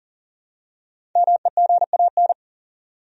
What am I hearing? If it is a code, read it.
Morse code, heard as a keyed tone, about 40 wpm, 700 Hz. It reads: MEGAN
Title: USA Common Names - Max 5 Letters - 40wpm